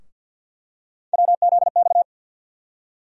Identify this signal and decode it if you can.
Morse code, heard as a keyed tone, about 46 wpm, 700 Hz. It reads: WZX
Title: USA Call Sign Postfixes - 3 Letters 50wpm